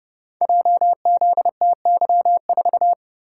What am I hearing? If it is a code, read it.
Morse code, heard as a keyed tone, about 31 wpm, 700 Hz. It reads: JZTY4
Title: Code Groups - 5 Letter:Numbers - 30wpm